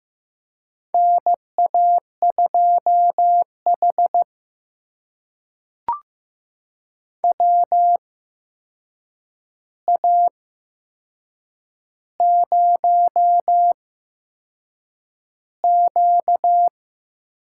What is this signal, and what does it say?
Morse code, heard as a keyed tone, about 15 wpm, spaced as for 6 wpm, 700 Hz. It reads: NA2H WA0Q
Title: USA Call Signs - 4 Characters - 8x Character Spacing - 15wpm